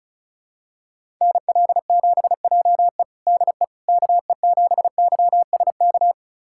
Morse code, heard as a keyed tone, about 36 wpm, 700 Hz. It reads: NL7JE DE KE7YSK
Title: USA Callsign DE Callsign 35wpm